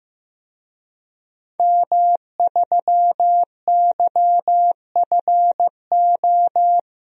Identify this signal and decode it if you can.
Morse code, heard as a keyed tone, about 15 wpm, 700 Hz. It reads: M3YFO
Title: DXCC-100 Call Signs - 5 Characters - 15wpm